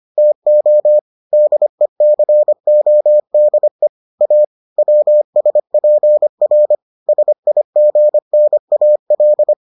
Morse code, heard as a keyed tone, about 25 wpm, 600 Hz. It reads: TO DECODE A WSPR SIGNAL